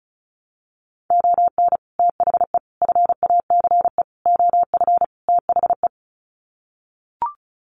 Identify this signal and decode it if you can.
Morse code, heard as a keyed tone, about 34 wpm, 700 Hz. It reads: ON THE FACE OF THE
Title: Sets of 5 Words Encore2 35wpm